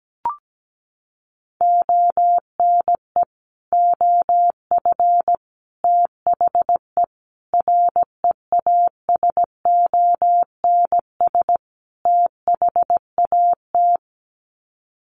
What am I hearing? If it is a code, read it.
Morse code, heard as a keyed tone, about 17 wpm, 700 Hz. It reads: ONE OF THE REASONS THAT
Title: Sets of 5 Words Encore 17wpm